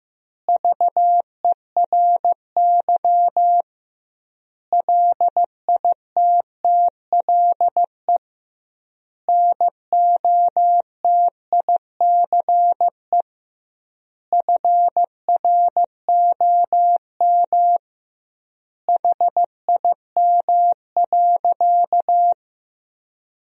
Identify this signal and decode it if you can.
Morse code, heard as a keyed tone, about 15 wpm, 700 Hz. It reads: VERY LITTLE NOTICE FROM HIM.
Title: 2x Word-Spacing - Sentences from Top 400 Words - 15wpm